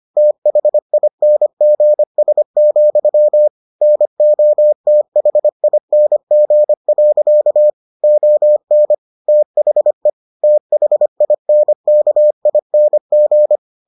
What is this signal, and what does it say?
Morse code, heard as a keyed tone, about 25 wpm, 600 Hz. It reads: THINGS, NOTHING. ON THE THINKING